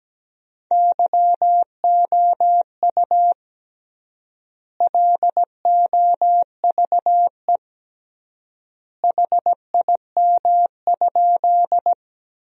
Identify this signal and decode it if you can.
Morse code, heard as a keyed tone, about 17 wpm, 700 Hz. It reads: YOU LOVE HIM?
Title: 3x Word-Spacing - Sentences from Top 400 Words - 17wpm